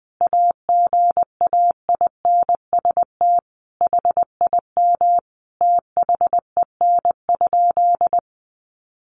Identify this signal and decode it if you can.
Morse code, heard as a keyed tone, about 20 wpm, 700 Hz. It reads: AGAINST HIM THEN?
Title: Sentences from Top 1300 Words 20wpm